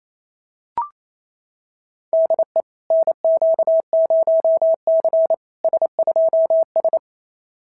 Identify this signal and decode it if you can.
Morse code, heard as a keyed tone, about 30 wpm, 650 Hz. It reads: DE NQ0C S2S
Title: SOTA QSO Practice 28wpm